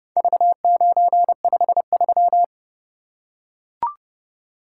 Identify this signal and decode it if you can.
Morse code, heard as a keyed tone, about 28 wpm, 700 Hz. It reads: V953